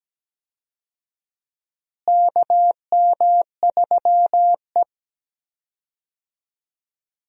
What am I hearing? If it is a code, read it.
Morse code, heard as a keyed tone, about 17 wpm, 700 Hz. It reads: KM3E
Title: USA Call Signs - 4 Characters - Repeated 3x - 17wpm